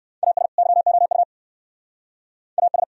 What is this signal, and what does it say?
Morse code, heard as a keyed tone, about 69 wpm, 700 Hz. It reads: AS YOU AS